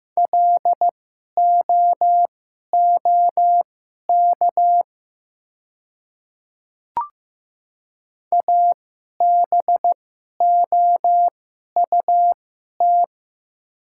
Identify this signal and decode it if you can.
Morse code, heard as a keyed tone, about 15 wpm, spaced as for 11 wpm, 700 Hz. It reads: LOOK ABOUT
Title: 2x Character Spacing - Top 100 Words - 15wpm